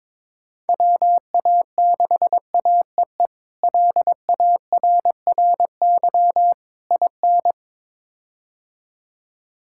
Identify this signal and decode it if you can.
Morse code, heard as a keyed tone, about 22 wpm, 700 Hz. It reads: WA6AEE LARRY IN
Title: K1USN SST QSO Practice - 22wpm